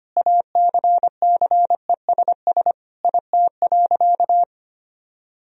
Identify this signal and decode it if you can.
Morse code, heard as a keyed tone, about 25 wpm, 700 Hz. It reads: ACCESS IT.
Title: Sentences from Top 1000 Words - Review Part 4 - 25wpm